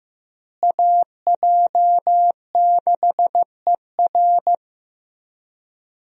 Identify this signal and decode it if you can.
Morse code, heard as a keyed tone, about 15 wpm, 700 Hz. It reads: AJ6ER
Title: USA Call Signs - 5 Characters - Repeated 3x - 15wpm